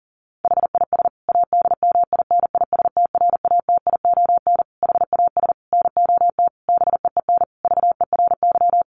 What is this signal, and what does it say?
Morse code, heard as a keyed tone, about 40 wpm, 700 Hz. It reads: HIS ADMINISTRATION HAS NOT BEEN VERY
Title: Sentences from Top 1300 Words 40wpm